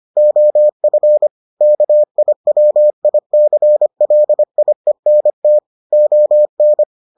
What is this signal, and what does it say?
Morse code, heard as a keyed tone, about 25 wpm, 600 Hz. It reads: OF KIWICLIENT ON